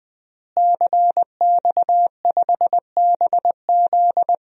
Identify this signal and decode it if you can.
Morse code, heard as a keyed tone, about 21 wpm, 700 Hz. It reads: CX5BZ
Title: DXCC-100 Call Signs - 5 Characters - Repeated 2x - 20wpm